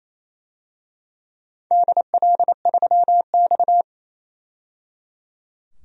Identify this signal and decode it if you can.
Morse code, heard as a keyed tone, about 27 wpm, 700 Hz. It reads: DL3X
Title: DXCC-100 Call Signs - 4 Characters - 28wpm